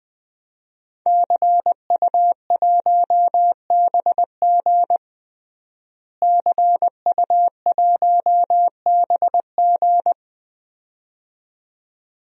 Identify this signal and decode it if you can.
Morse code, heard as a keyed tone, about 21 wpm, 700 Hz. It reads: CU1BG CU1BG
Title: DXCC-100 Call Signs - 5 Characters - Repeated 2x - 20wpm